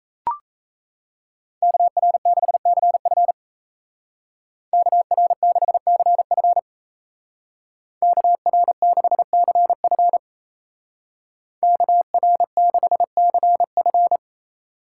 Morse code, heard as a keyed tone, about 37 wpm, 700 Hz. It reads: KR6CF KR6CF KR6CF KR6CF